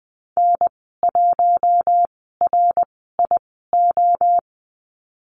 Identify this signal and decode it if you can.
Morse code, heard as a keyed tone, about 19 wpm, spaced as for 14 wpm, 700 Hz. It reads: N1RIO
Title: 076 Lesson 13g - Letter L - Call Signs 20wpm